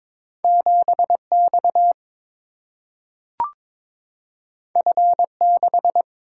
Morse code, heard as a keyed tone, about 22 wpm, 700 Hz. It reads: 7X F6
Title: Single-Letter Single-Number 22wpm